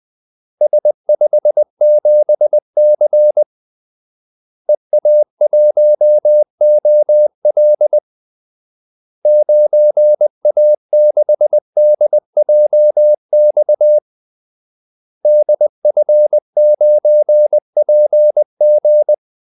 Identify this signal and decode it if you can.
Morse code, heard as a keyed tone, about 20 wpm, 600 Hz. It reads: S57C EA1OL 9A6DJX DF9PG
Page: Kaldesignaler 20 wpm